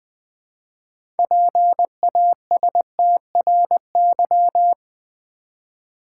Morse code, heard as a keyed tone, about 20 wpm, 690 Hz. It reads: PASTRY
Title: Easy Words with Hints - 20wpm